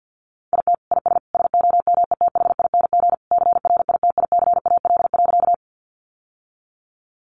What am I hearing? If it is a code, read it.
Morse code, heard as a keyed tone, about 50 wpm, 700 Hz. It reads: IT IS SOMETHING CRITICAL.
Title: Sentences from Top 1100 Words 50wpm